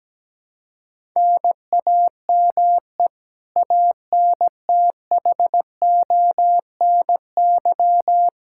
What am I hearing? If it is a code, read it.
Morse code, heard as a keyed tone, about 17 wpm, 700 Hz. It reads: NAME ANTHONY
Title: Practice QSO - Exchange 1 - Beginning - 17wpm